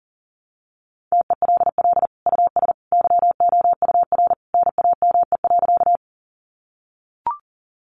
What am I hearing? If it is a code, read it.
Morse code, heard as a keyed tone, about 40 wpm, 700 Hz. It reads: TELL US YOUR NAME.
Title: Sentences from Top 200 Words 40wpm